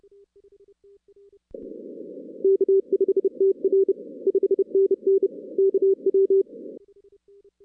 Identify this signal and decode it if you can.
Morse code, heard as a keyed tone, about 34 wpm, 410 Hz. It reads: K5TR 5NN KW
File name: 2016 arrl dx cw / 2016-ARRL-DX-CW-K5TR- - 0-46